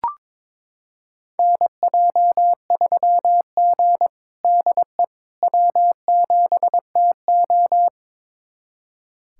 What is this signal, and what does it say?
Morse code, heard as a keyed tone, about 22 wpm, 700 Hz. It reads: NJ3G DE W7TO
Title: USA Callsign DE Callsign - 4 Characters - 22wpm